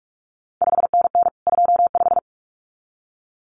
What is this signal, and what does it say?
Morse code, heard as a keyed tone, about 46 wpm, 700 Hz. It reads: 5NN 25
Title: CQ WW DX Contest Exchange - 45wpm